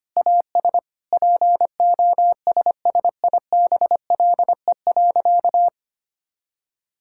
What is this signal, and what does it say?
Morse code, heard as a keyed tone, about 25 wpm, 700 Hz. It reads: AS POSSIBLE.
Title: Sentences from Top 1000 Words - Review Part 5 - 25wpm